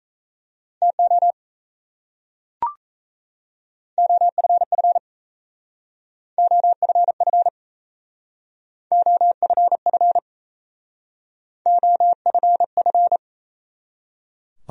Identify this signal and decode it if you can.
Morse code, heard as a keyed tone, about 36 wpm, 700 Hz. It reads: TO OFF OFF OFF OFF